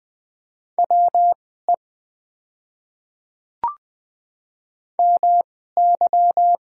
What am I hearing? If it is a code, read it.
Morse code, heard as a keyed tone, about 20 wpm, spaced as for 14 wpm, 700 Hz. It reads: WE MY
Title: 2x Character Spacing - Top 200 Words - 20wpm